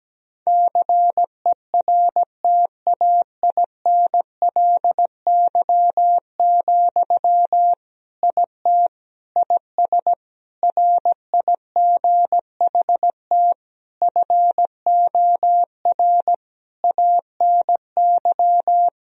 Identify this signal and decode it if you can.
Morse code, heard as a keyed tone, about 17 wpm, 700 Hz. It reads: CERTAINLY, IT IS RIGHT FOR ANY
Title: Sentences from Top 1000 Words - Review Part 2 - 17wpm